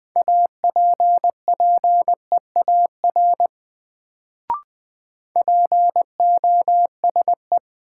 Morse code, heard as a keyed tone, about 20 wpm, 700 Hz. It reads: APPEAR POSE
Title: Sentences from Top 1000 Words - Review Part 2 - New Word Drill - 20wpm